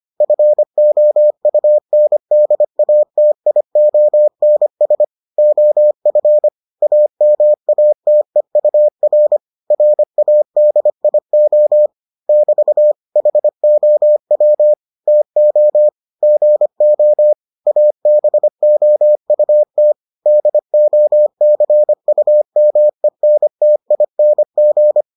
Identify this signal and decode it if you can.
Morse code, heard as a keyed tone, about 25 wpm, 600 Hz. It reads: FOUNDATIONS OF AMATEUR RADIO <BT> HOW TO GO ABOUT DOCUMENTING